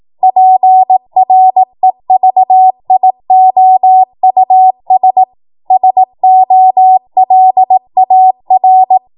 Morse code, heard as a keyed tone, about 18 wpm, 750 Hz. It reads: PREVIOUS SOLAR
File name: Morse / 18WPM